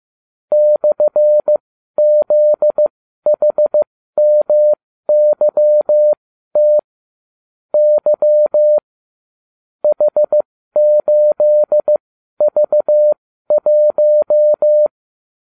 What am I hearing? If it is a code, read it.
Morse code, heard as a keyed tone, about 15 wpm, spaced as for 12 wpm, 600 Hz. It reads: /ZHMYT Y H8V1